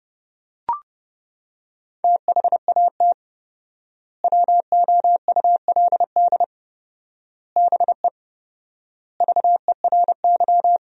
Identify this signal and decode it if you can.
Morse code, heard as a keyed tone, about 29 wpm, 700 Hz. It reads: THAT WOULD BE VERY
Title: 4x Word-Spacing - Sentences from Top 200 Words - 30wpm